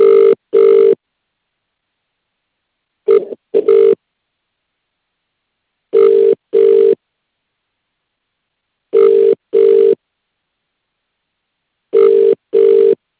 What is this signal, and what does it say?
Ringback tone (UK), 400 + 450 Hz.